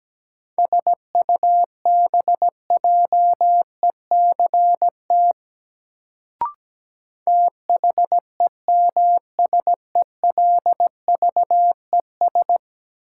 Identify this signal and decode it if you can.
Morse code, heard as a keyed tone, about 17 wpm, 700 Hz. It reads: SUBJECT THEMSELVES
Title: Sentences from Top 800 Words - New Word Drill - 17wpm